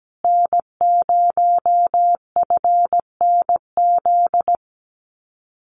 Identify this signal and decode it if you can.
Morse code, heard as a keyed tone, about 17 wpm, 700 Hz. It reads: N0FNZ
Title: Common QSO Phrases Encore 17wpm